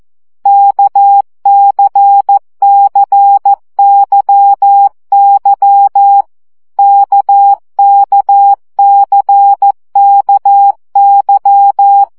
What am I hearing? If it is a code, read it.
Morse code, heard as a keyed tone, about 14 wpm, 800 Hz. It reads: KCCYY KKCKY
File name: CW gyakorlas k c y 12 baud